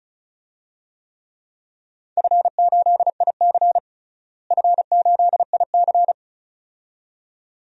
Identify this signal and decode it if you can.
Morse code, heard as a keyed tone, about 35 wpm, 700 Hz. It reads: F8IC F8IC
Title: DXCC-100 Call Signs - 4 Characters - Repeated 2x - 35wpm